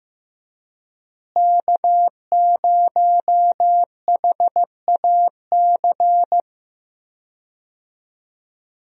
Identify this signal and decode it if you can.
Morse code, heard as a keyed tone, about 15 wpm, 700 Hz. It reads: K0HAC
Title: USA Call Signs - 5 Characters - Repeated 3x - 15wpm